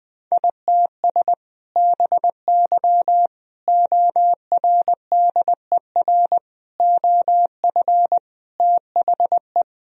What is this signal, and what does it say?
Morse code, heard as a keyed tone, about 20 wpm, 700 Hz. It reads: ITS BY ORDER OF THE
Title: Sentences from Top 1000 Words - Review Part 3 - 20wpm